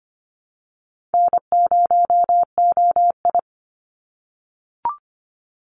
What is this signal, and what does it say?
Morse code, heard as a keyed tone, about 25 wpm, 690 Hz. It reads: N0OI
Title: NA SOTA Activator Callsigns 25wpm